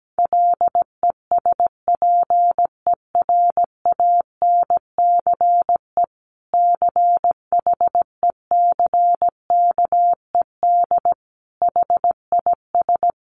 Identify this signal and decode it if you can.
Morse code, heard as a keyed tone, about 17 wpm, 700 Hz. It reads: LESPERANCE CHECKED HIS
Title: A Sound Of Thunder 17wpm